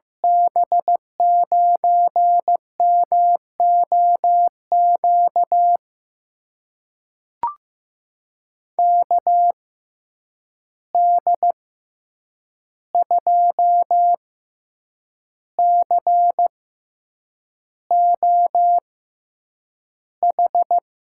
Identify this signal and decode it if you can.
Morse code, heard as a keyed tone, about 15 wpm, spaced as for 7 wpm, 700 Hz. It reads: B9MOQ KD2COH
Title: USA Call Signs - 6 Characters - 6x Character Spacing - 15wpm